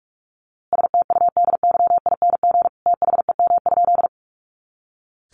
Morse code, heard as a keyed tone, about 44 wpm, 700 Hz. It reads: STUDYING THEM?
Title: A Sound Of Thunder 45wpm